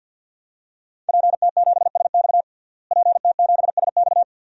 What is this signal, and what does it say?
Morse code, heard as a keyed tone, about 52 wpm, 700 Hz. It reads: PT7SX PT7SX
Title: DXCC-100 Call Signs - 5 Characters - Repeated 2x - 50wpm